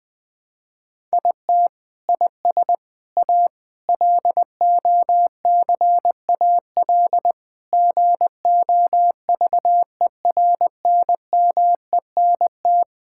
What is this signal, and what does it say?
Morse code, heard as a keyed tone, about 20 wpm, 700 Hz. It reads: IT IS A LOCAL GOVERNMENT
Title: Sentences from Top 1000 Words - Review Part 5 - 20wpm